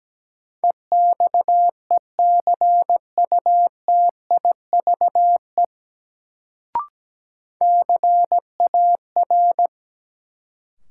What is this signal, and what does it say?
Morse code, heard as a keyed tone, about 17 wpm, 700 Hz. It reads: EXECUTIVE CAR
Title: Sets of 5 Words Encore - New Word Drill - 17wpm